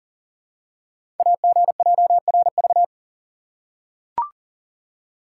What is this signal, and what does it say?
Morse code, heard as a keyed tone, about 42 wpm, 700 Hz. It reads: AGJRV
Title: Code Groups - 5 Letter:Numbers - 40wpm